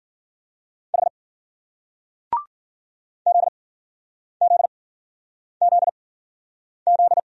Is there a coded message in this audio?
Yes: Morse code from 0.9 to 7.2 s, heard as a keyed tone, about 50 wpm, spaced as for 9 wpm, 700 Hz. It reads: H ZZZZ